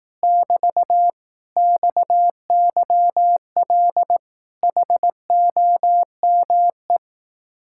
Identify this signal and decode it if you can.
Morse code, heard as a keyed tone, about 18 wpm, 700 Hz. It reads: <BT> XYL HOME
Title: Speed-Racing Common QSO Phrases Encore - 15wpm